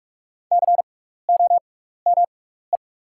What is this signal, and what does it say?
Morse code, heard as a keyed tone, about 44 wpm, spaced as for 15 wpm, 690 Hz. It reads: COME
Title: 6x Character Spacing - Top 100 Words - 45wpm